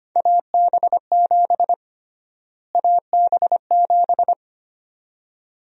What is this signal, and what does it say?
Morse code, heard as a keyed tone, about 25 wpm, 700 Hz. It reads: AB7 AB7
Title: USA Call Sign Prefixes - Repeated 2x - 25wpm